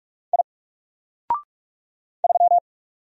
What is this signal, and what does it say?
Morse code, heard as a keyed tone, about 46 wpm, 700 Hz. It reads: I 3
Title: Single Letter-Number 45wpm